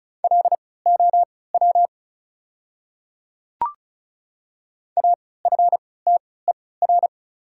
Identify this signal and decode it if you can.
Morse code, heard as a keyed tone, about 36 wpm, spaced as for 20 wpm, 690 Hz. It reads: LOW AFTER
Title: 3x Character Spacing - Top 200 Words - 35wpm